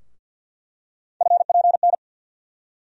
Morse code, heard as a keyed tone, about 48 wpm, 700 Hz. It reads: FPN
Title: USA Call Sign Postfixes - 3 Letters 50wpm